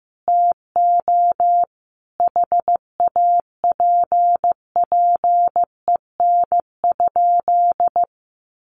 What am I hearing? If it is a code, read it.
Morse code, heard as a keyed tone, about 15 wpm, 700 Hz. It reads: TO HAPPEN?
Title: Sentences from Top 1100 Words 15wpm